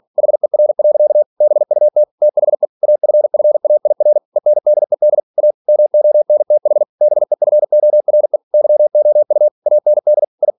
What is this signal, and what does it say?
Morse code, heard as a keyed tone, about 46 wpm, 600 Hz. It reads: HER. BUT THE AFFAIR ENDED A MONTH BEFORE YOU AND I
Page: Never 47wpm